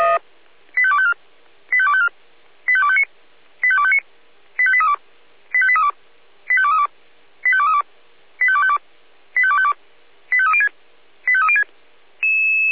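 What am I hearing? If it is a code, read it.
Five-tone selective call (ZVEI), heard as quick runs of five tones, tones of about 70 ms. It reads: 85254 85254 85259 85259 85832 85832 85232 85232 85252 85252 85306 85306
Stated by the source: ZVEI five-tone selective call received by SDR.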